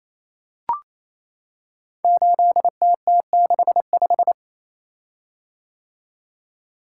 Morse code, heard as a keyed tone, about 27 wpm, 700 Hz. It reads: 8TT65
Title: Code Groups - 5 Letter:Numbers - 28wpm